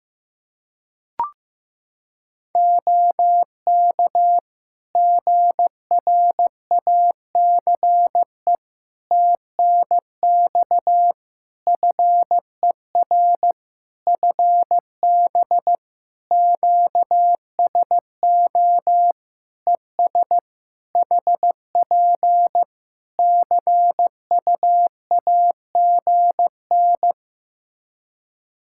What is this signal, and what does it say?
Morse code, heard as a keyed tone, about 15 wpm, 700 Hz. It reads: OK GRACE TNX FER FB QSO ES HP CUAGN
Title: Practice QSO - Exchange 4 - Ending - 15wpm